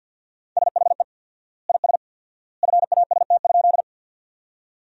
Morse code, heard as a keyed tone, about 50 wpm, 700 Hz. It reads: SHE IS FAST?
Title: 4x Word-Spacing - Sentences from Top 400 Words - 50wpm